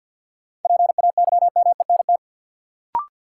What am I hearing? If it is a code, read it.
Morse code, heard as a keyed tone, about 49 wpm, 700 Hz. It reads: PAYMENT
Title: Sentences from Top 1200 Words - New Word Drill - 50wpm